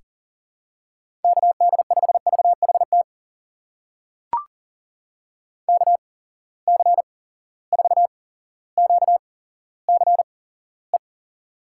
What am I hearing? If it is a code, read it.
Morse code, heard as a keyed tone, about 42 wpm, spaced as for 17 wpm, 700 Hz. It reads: KD5VHT KC4QCE
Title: USA Call Signs - 6 Characters - 8x Character Spacing - 40wpm